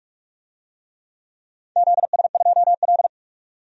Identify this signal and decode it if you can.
Morse code, heard as a keyed tone, about 42 wpm, 700 Hz. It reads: ZS2L
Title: DXCC-100 Call Signs - 4 Characters - 45wpm